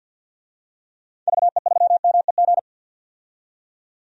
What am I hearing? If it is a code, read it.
Morse code, heard as a keyed tone, about 50 wpm, 700 Hz. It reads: VE3MEG
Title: DXCC-100 Call Signs - 6 Characters - 50wpm